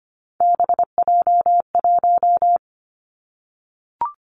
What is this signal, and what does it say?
Morse code, heard as a keyed tone, about 25 wpm, 700 Hz. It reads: BJ1
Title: Three Challenging Letter-Number Combinations 25wpm